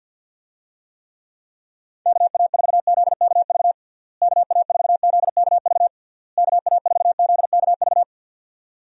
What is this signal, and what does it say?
Morse code, heard as a keyed tone, about 46 wpm, 700 Hz. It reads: KA4ZKV KA4ZKV KA4ZKV
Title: USA Call Signs - 6 Characters - Repeated 3x - 50wpm